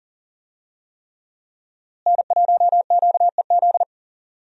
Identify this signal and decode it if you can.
Morse code, heard as a keyed tone, about 37 wpm, 690 Hz. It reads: N1QEZ
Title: USA Call Signs - 5 Characters - Repeated 3x - 40wpm